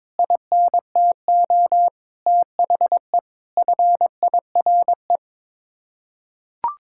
Morse code, heard as a keyed tone, about 22 wpm, 700 Hz. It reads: INTO THE FIRE
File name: Sets of 9 Words - 22wpm